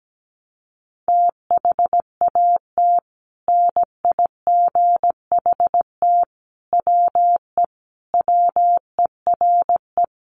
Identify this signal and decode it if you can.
Morse code, heard as a keyed tone, about 17 wpm, 700 Hz. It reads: THAT NIGHT WE WERE
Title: Sentences from Top 300 Words 17wpm